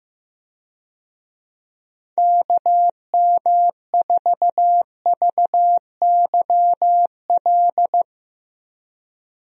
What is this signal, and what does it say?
Morse code, heard as a keyed tone, about 15 wpm, 700 Hz. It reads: KM4VYL
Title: USA Call Signs - 6 Characters - 15wpm